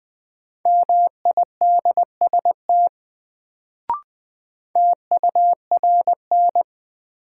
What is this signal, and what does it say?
Morse code, heard as a keyed tone, about 20 wpm, 700 Hz. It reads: MIDST TURN
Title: Sets of 4 Words - New Word Drill - 20wpm